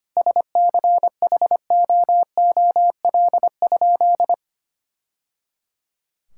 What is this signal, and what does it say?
Morse code, heard as a keyed tone, about 25 wpm, 700 Hz. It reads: SCHOOL?
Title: Sentences from Top 1000 Words - Review Part 5 - 25wpm